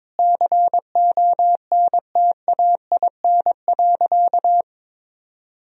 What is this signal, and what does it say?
Morse code, heard as a keyed tone, about 22 wpm, 700 Hz. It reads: CONTAIN.
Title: Sentences from Top 1000 Words - Review Part 2 - 22wpm